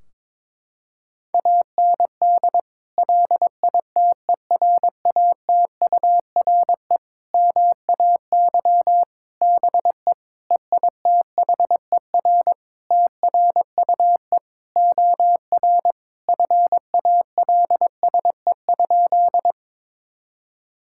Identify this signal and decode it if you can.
Morse code, heard as a keyed tone, about 22 wpm, 700 Hz. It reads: AND LITERATURE MAY BE EITHER TRUE OR FALSE?
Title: Sentences from Top 1600 Words 22wpm